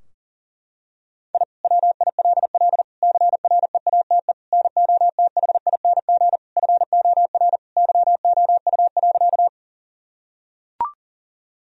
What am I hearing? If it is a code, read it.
Morse code, heard as a keyed tone, about 40 wpm, 700 Hz. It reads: I WILL CREATE NOTHING FOR YOU.
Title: Sentences from Top 1000 Words - Review Part 3 - 40wpm